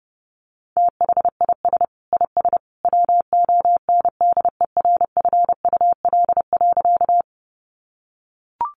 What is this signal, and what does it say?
Morse code, heard as a keyed tone, about 30 wpm, 700 Hz. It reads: THIS IS WONDERFUL.
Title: Sentences from Top 1100 Words 30wpm